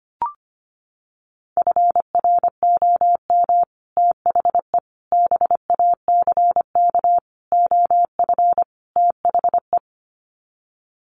Morse code, heard as a keyed tone, about 25 wpm, 700 Hz. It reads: FROM THE BACK OF THE
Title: Sets of 5 Words Encore 25wpm